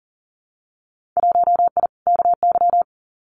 Morse code, heard as a keyed tone, about 40 wpm, 700 Hz. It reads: 1I KY
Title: Winter Field Day Exchanges 40wpm